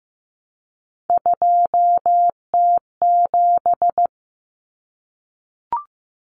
Morse code, heard as a keyed tone, about 15 wpm, 700 Hz. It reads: 2T7
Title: Common Cut Numbers - 3-Digit 15wpm